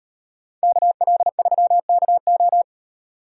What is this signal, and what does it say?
Morse code, heard as a keyed tone, about 38 wpm, 700 Hz. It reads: KL3KO